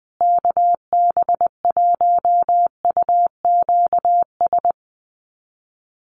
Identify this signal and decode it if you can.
Morse code, heard as a keyed tone, about 20 wpm, 700 Hz. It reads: KB1UQS
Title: USA Call Signs Repeated 3x 20wpm